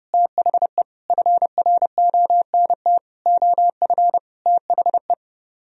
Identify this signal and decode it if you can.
Morse code, heard as a keyed tone, about 35 wpm, 700 Hz. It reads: THE FRONT OF THE
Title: Speed-Racing - Sets of 5 Words 30wpm